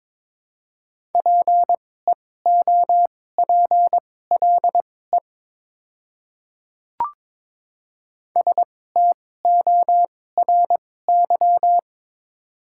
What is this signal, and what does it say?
Morse code, heard as a keyed tone, about 22 wpm, spaced as for 16 wpm, 700 Hz. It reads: PEOPLE STORY
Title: 2x Character Spacing - Top 300 Words - 22wpm